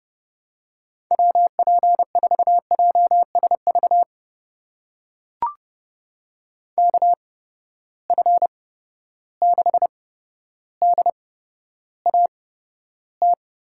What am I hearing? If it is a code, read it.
Morse code, heard as a keyed tone, about 31 wpm, spaced as for 13 wpm, 700 Hz. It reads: WP4JSV KF6DAT
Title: USA Call Signs - 6 Characters - 8x Character Spacing - 30wpm